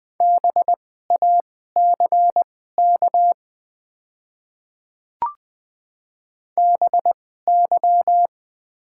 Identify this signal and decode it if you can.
Morse code, heard as a keyed tone, about 20 wpm, spaced as for 14 wpm, 700 Hz. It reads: BACK BY
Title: 2x Character Spacing - Top 300 Words - 20wpm